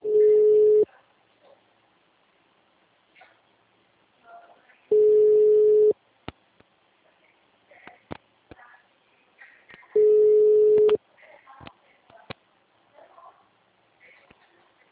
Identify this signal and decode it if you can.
Ringback tone (Europe, 425 Hz), with speech.